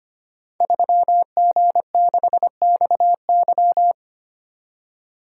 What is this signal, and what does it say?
Morse code, heard as a keyed tone, about 25 wpm, 700 Hz. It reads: 3G6XY
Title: Code Groups - 5 Letter:Numbers - 25wpm